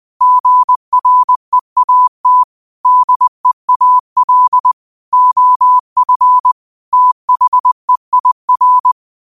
Morse code, heard as a keyed tone, about 20 wpm, 1000 Hz. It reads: GREAT DEAL OF THEIR
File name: qotdfiles / 20wpm